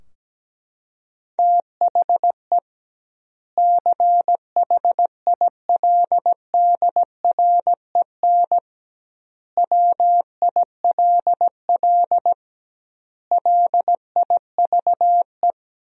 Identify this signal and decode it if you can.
Morse code, heard as a keyed tone, about 17 wpm, 700 Hz. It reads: THE CHILDREN WILL LIVE
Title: 2x Word-Spacing - Sentences from Top 300 Words - 17wpm